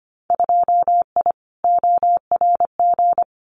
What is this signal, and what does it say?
Morse code, heard as a keyed tone, about 25 wpm, 700 Hz. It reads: 2I ORG
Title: Winter Field Day Exchanges Encore 25wpm